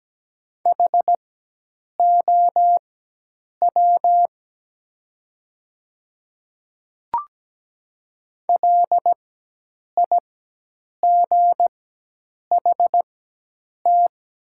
Morse code, heard as a keyed tone, about 17 wpm, spaced as for 8 wpm, 700 Hz. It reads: HOW LIGHT
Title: 4x Character Spacing - Top 300 Words - 17wpm